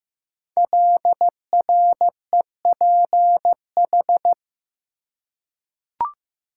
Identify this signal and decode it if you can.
Morse code, heard as a keyed tone, about 15 wpm, 700 Hz. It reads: LREPH
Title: Code Groups - 5 Letters - 15wpm